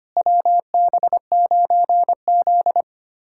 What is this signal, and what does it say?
Morse code, heard as a keyed tone, about 25 wpm, 700 Hz. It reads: WB9Z
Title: USA Callsign DE Callsign - 4 Characters - 25wpm